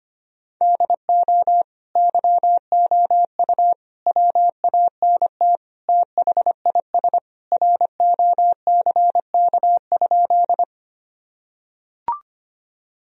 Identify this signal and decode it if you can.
Morse code, heard as a keyed tone, about 25 wpm, 700 Hz. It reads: DO YOU WANT THIS ROCK?
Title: Sentences from Top 1000 Words - Review Part 2 - 25wpm